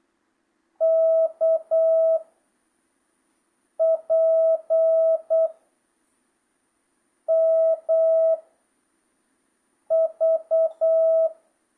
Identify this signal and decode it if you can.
Morse code, heard as a keyed tone, about 8 wpm, 660 Hz. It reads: K P M V
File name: jh8cbh / 2025cw